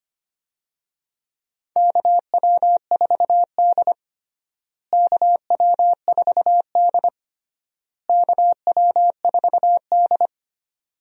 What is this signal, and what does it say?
Morse code, heard as a keyed tone, about 25 wpm, 700 Hz. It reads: KW4D KW4D KW4D
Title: USA Call Signs - 4 Characters - Repeated 3x - 25wpm